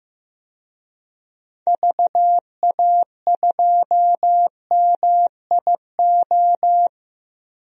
Morse code, heard as a keyed tone, about 15 wpm, 700 Hz. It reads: VA2MIO
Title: DXCC-100 Call Signs - 6 Characters - 15wpm